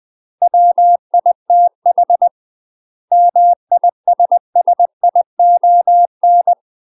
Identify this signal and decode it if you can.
Morse code, heard as a keyed tone, about 20 wpm, 700 Hz. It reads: WITH MISSION
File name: UK MASTER QSOs TRAINING / 20WPM / UK MASTER QSOs 0039